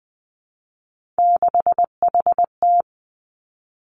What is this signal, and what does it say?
Morse code, heard as a keyed tone, about 20 wpm, 700 Hz. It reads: 6HT
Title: Three Challenging Letter-Number Combinations 20wpm